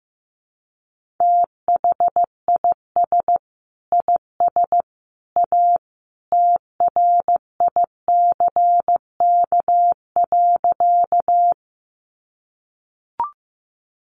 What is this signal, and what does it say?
Morse code, heard as a keyed tone, about 15 wpm, 700 Hz. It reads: THIS IS A TRICK.
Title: Sentences from Top 1500 Words 15wpm